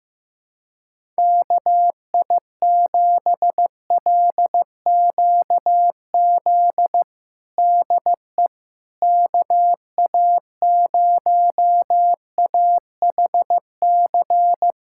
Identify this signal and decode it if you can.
Morse code, heard as a keyed tone, about 15 wpm, 700 Hz. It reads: KI7LQZ DE KA0AHC
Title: USA Callsign DE Callsign 15wpm